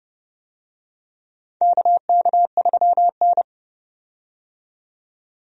Morse code, heard as a keyed tone, about 30 wpm, 700 Hz. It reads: KK3N
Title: USA Call Signs - 4 Characters - 30wpm